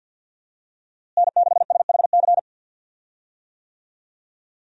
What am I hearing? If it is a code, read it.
Morse code, heard as a keyed tone, about 52 wpm, 700 Hz. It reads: N6SHC
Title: USA Call Signs - 5 Characters - Repeated 3x - 50wpm